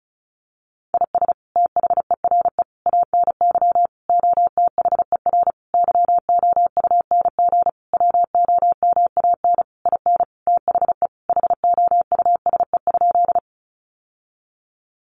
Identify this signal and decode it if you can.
Morse code, heard as a keyed tone, about 34 wpm, 700 Hz. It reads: IS THERE ANY OTHER YOUNG WOMAN IN THE HOUSE?
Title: Sentences from Top 1100 Words 35wpm